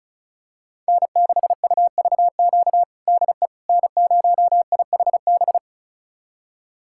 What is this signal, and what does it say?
Morse code, heard as a keyed tone, about 36 wpm, 700 Hz. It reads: N6UVQ DE N0IHB
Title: USA Callsign DE Callsign - 5 Characters - 35wpm